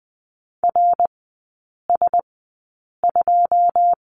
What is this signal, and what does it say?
Morse code, heard as a keyed tone, about 20 wpm, spaced as for 8 wpm, 700 Hz. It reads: RS2
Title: 107 Lesson 18e - Letter m - Three Characters 20wpm